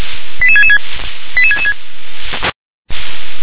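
Five-tone selective call (ZVEI), heard as quick runs of five tones, tones of about 70 ms. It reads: 88666 88666
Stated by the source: ZVEI five-tone selective call received by SDR.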